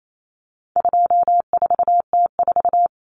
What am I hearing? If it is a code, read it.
Morse code, heard as a keyed tone, about 28 wpm, 700 Hz. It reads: 24T4
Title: Common Cut Numbers - 4-Digit 28wpm